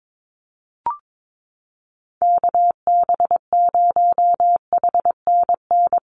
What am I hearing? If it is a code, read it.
Morse code, heard as a keyed tone, about 22 wpm, 700 Hz. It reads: KB0HNN
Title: USA Call Signs Repeated 2x 22wpm